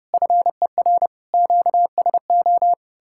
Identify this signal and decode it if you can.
Morse code, heard as a keyed tone, about 37 wpm, 700 Hz. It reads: FER QSO
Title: Speed-Racing Common QSO Phrases - 30wpm